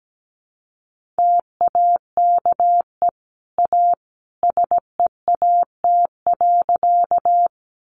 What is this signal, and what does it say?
Morse code, heard as a keyed tone, about 17 wpm, 700 Hz. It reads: TAKE A SEAT.
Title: Sentences from Top 1500 Words 17wpm